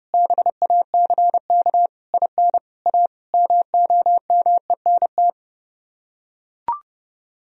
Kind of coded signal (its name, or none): Morse code